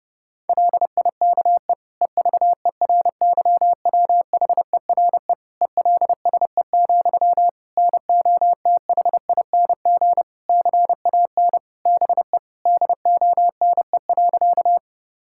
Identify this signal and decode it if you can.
Morse code, heard as a keyed tone, about 32 wpm, 700 Hz. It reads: LIKE EVERYWHERE ELSE, NOTHING CAN BE DONE.
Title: Sentences from Top 1600 Words 30wpm